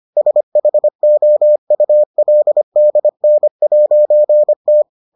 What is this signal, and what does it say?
Morse code, heard as a keyed tone, about 25 wpm, 600 Hz. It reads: SHOULDN'T